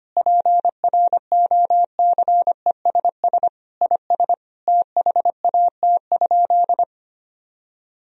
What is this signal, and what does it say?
Morse code, heard as a keyed tone, about 25 wpm, 700 Hz. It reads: PROCESS IS THAT?
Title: Sentences from Top 1000 Words - Review Part 4 - 25wpm